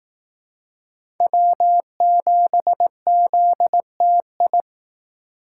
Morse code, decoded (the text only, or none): W7ZTI